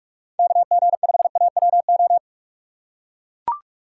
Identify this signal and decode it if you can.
Morse code, heard as a keyed tone, about 42 wpm, 700 Hz. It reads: KG5AWO
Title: USA Call Signs - 6 Characters - 4x Character Spacing - 45wpm